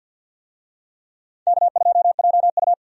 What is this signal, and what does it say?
Morse code, heard as a keyed tone, about 51 wpm, 700 Hz. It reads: K2JU
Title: USA Call Signs - 4 Characters - Repeated 2x - 50wpm